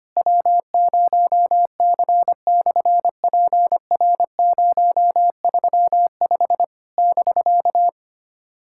Morse code, heard as a keyed tone, about 24 wpm, 700 Hz. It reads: W0C/PR035 <BK>
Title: SOTA QSO Practice 25wpm